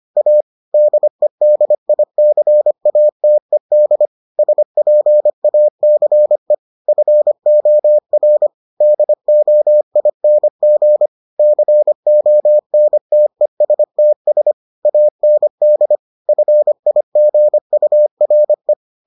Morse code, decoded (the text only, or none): A DEDICATED SPACE FOR DOING CONTESTS AND FIGURE